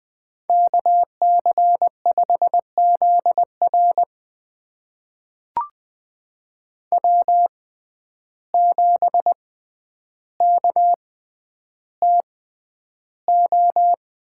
Morse code, heard as a keyed tone, about 20 wpm, spaced as for 10 wpm, 700 Hz. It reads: KC5ZR W7KTO
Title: USA Call Signs - 6x Character Spacing - 20wpm